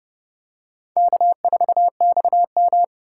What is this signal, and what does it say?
Morse code, heard as a keyed tone, about 30 wpm, 700 Hz. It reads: K4XM
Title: USA Call Signs - 4 Characters - 2x Character Spacing - 30wpm